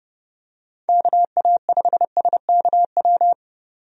Morse code, heard as a keyed tone, about 30 wpm, 700 Hz. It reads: KA5SKW
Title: USA Call Signs - 6 Characters - 30wpm